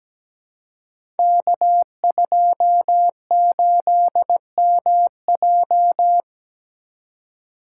Morse code, heard as a keyed tone, about 17 wpm, 700 Hz. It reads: K28MJ
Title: Code Groups - 5 Letter:Numbers - 17wpm